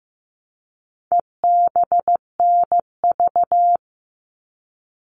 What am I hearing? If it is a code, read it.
Morse code, heard as a keyed tone, about 15 wpm, 700 Hz. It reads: EBNV
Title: All Cut Numbers - 4-Digit 15wpm